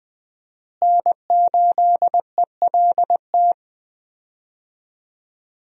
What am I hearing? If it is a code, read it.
Morse code, heard as a keyed tone, about 20 wpm, 700 Hz. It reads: N8ELT
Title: USA Call Signs - 5 Characters - Repeated 3x - 20wpm